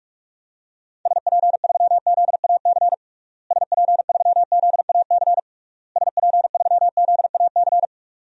Morse code, heard as a keyed tone, about 46 wpm, 700 Hz. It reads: SP3ZAC SP3ZAC SP3ZAC
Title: DXCC-100 Call Signs - 6 Characters - Repeated 3x - 45wpm